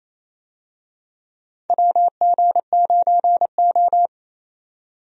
Morse code, heard as a keyed tone, about 27 wpm, 700 Hz. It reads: WG9O
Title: USA Call Signs - 4 Characters - 28wpm